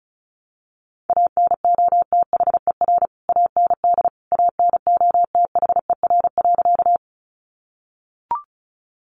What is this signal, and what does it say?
Morse code, heard as a keyed tone, about 34 wpm, 700 Hz. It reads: ANOTHER AND ANOTHER.
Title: Sentences from Top 600 Words 35wpm